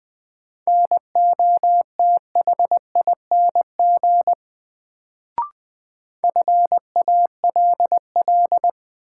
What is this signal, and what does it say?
Morse code, heard as a keyed tone, about 20 wpm, 700 Hz. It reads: NOTHING FALL
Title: Sentences from Top 500 Words - New Word Drill - 20wpm